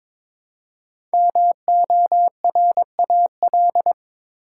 Morse code, heard as a keyed tone, about 23 wpm, 700 Hz. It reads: MORAL
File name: 5 Letter Words - First 3 Letters Given - 22wpm